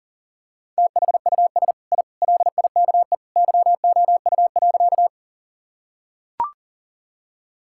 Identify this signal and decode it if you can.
Morse code, heard as a keyed tone, about 40 wpm, 700 Hz. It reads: THUS I LIKE YOU.
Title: Sentences from Top 1000 Words - Review Part 5 - 40wpm